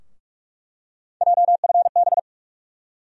Morse code, heard as a keyed tone, about 46 wpm, 700 Hz. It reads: JFB
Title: USA Call Sign Postfixes - 3 Letters 45wpm